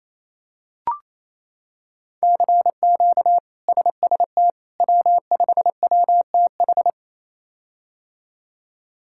Morse code, heard as a keyed tone, about 28 wpm, 700 Hz. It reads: CQ SST W5WTH
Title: K1USN SST QSO Practice - 28wpm